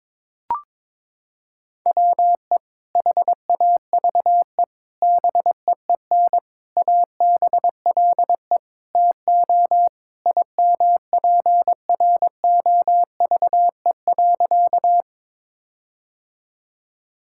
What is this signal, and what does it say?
Morse code, heard as a keyed tone, about 22 wpm, 700 Hz. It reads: WE HAVE BEEN ABLE TO IMPROVE.
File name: practice / 22wpm / Sentences with 4-Word Phrases - 22wpm